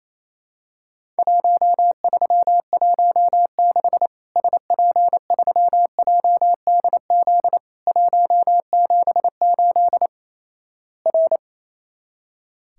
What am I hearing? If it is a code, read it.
Morse code, heard as a keyed tone, about 28 wpm, 700 Hz. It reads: 1316 SP3JDZ 178 R
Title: Worked All Europe DX Contest - QTC Exchange - 28wpm